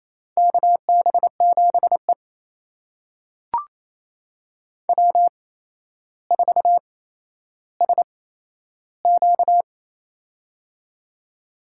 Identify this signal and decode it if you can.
Morse code, heard as a keyed tone, about 28 wpm, spaced as for 12 wpm, 700 Hz. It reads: KB7E W4SQ
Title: USA Call Signs - 4 Characters - 8x Character Spacing - 28wpm